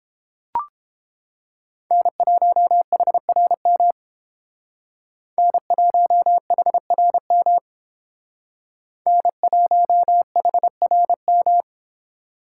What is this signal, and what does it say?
Morse code, heard as a keyed tone, about 31 wpm, 700 Hz. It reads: N1HRM N1HRM N1HRM